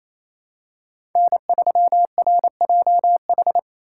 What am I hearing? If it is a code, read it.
Morse code, heard as a keyed tone, about 28 wpm, 700 Hz. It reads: N3RJH
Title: USA Call Signs - 3x Character Spacing - 28wpm